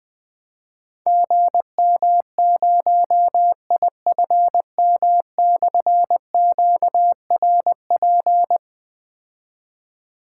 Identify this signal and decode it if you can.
Morse code, heard as a keyed tone, about 21 wpm, 700 Hz. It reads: GM0IFM/QRP
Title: Difficult Call Signs 20wpm